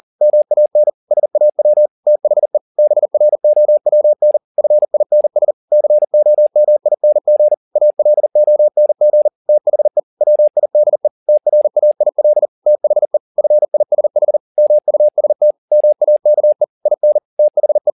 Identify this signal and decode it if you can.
Morse code, heard as a keyed tone, about 40 wpm, 600 Hz. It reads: MAN SAW THE BROWN FINS COMING ALONG THE WIDE TRAIL THE FISH MUST MAKE IN THE
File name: the old man and the sea 2 40wpm